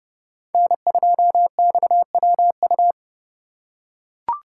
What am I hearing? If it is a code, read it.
Morse code, heard as a keyed tone, about 28 wpm, 700 Hz. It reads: N2XWU